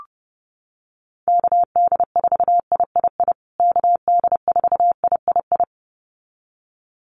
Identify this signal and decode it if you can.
Morse code, heard as a keyed tone, about 30 wpm, 700 Hz. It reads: KD4III KD4III
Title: USA Call Signs Repeated 2x 30wpm